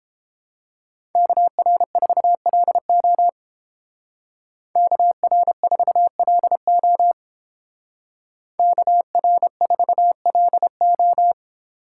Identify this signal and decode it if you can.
Morse code, heard as a keyed tone, about 29 wpm, 700 Hz. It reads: KR4LO KR4LO KR4LO